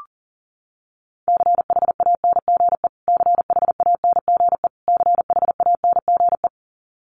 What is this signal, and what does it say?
Morse code, heard as a keyed tone, about 40 wpm, 700 Hz. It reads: CHANGE CHANGE CHANGE
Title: Top 500 Words Repeated 40wpm